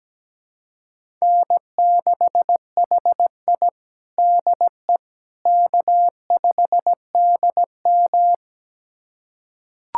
Morse code, heard as a keyed tone, about 17 wpm, 700 Hz. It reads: N6HI DE K5DM
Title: USA Callsign DE Callsign - 4 Characters - 17wpm